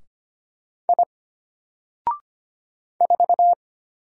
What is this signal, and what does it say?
Morse code, heard as a keyed tone, about 25 wpm, 700 Hz. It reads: I 4
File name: Single-Letter-Number-25wpm